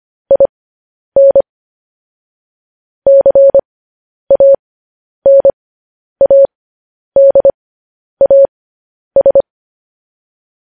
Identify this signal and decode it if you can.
Morse code, heard as a keyed tone, about 25 wpm, spaced as for 10 wpm, 550 Hz. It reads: IN CANADAS
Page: news headlines in morse code at 10 wpm